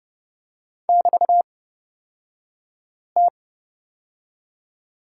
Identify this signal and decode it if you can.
Morse code, heard as a keyed tone, about 30 wpm, spaced as for 5 wpm, 700 Hz. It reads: <BT>T
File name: Lesson 36d - BT - Two Characters 30wpm